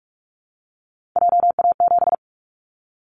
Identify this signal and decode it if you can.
Morse code, heard as a keyed tone, about 46 wpm, 700 Hz. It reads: JA7
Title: Callsign Prefixes for Top 25 DXCC - 45wpm